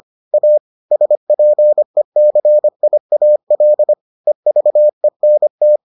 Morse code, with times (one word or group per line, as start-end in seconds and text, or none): 0.3-0.6 A
0.9-3.9 SPECIAL
4.3-5.8 EVENT